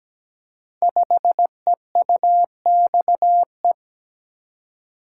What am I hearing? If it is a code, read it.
Morse code, heard as a keyed tone, about 17 wpm, 700 Hz. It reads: 5EUXE
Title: Code Groups - 5 Letter:Numbers - 17wpm